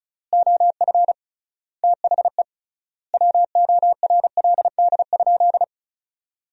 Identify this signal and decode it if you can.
Morse code, heard as a keyed tone, about 36 wpm, 700 Hz. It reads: OF THE WORLD?
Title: 3x Word-Spacing - Sentences from Top 200 Words - 35wpm